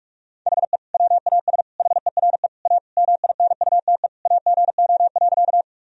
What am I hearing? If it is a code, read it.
Morse code, heard as a keyed tone, about 45 wpm, 700 Hz. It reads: HE WAS HERE A MINUTE AGO.
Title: Sentences from Top 1000 Words - Review Part 3 - 45wpm